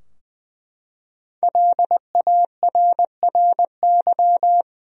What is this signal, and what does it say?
Morse code, heard as a keyed tone, about 20 wpm, 700 Hz. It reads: LARRY
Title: USA Common Names - 20wpm